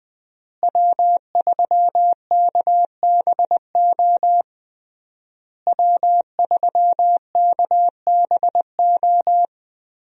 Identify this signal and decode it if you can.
Morse code, heard as a keyed tone, about 20 wpm, 700 Hz. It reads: W3KBO W3KBO
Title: USA Call Signs - 5 Characters - Repeated 3x - 20wpm